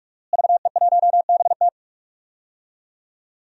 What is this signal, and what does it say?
Morse code, heard as a keyed tone, about 41 wpm, 700 Hz. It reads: VE1BT
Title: DXCC-100 Call Signs - 5 Characters - Repeated 2x - 45wpm